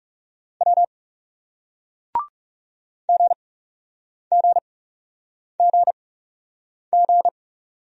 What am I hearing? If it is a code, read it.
Morse code, heard as a keyed tone, about 41 wpm, spaced as for 8 wpm, 700 Hz. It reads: W GGGG